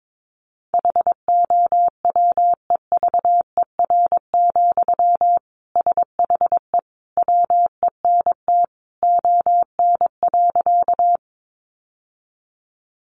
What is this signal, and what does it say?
Morse code, heard as a keyed tone, about 22 wpm, 700 Hz. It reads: HOWEVER, SHE WENT ON.
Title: Sentences from Top 600 Words 22wpm